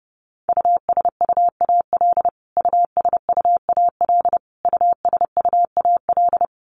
Morse code, heard as a keyed tone, about 30 wpm, 700 Hz. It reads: USUAL USUAL USUAL
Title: Top 500 Words Repeated 30wpm